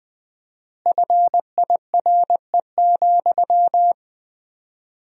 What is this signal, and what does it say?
Morse code, heard as a keyed tone, about 20 wpm, 700 Hz. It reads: FIRE,
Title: 4x Word-Spacing - Sentences from Top 400 Words - 20wpm